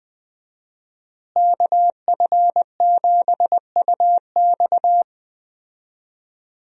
Morse code, heard as a keyed tone, about 20 wpm, 700 Hz. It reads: KF7UX
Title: USA Call Signs - 5 Characters - Repeated 3x - 20wpm